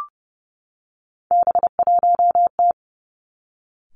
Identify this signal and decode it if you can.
Morse code, heard as a keyed tone, about 30 wpm, 700 Hz. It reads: B1T
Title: Three Challenging Letter-Number Combinations 30wpm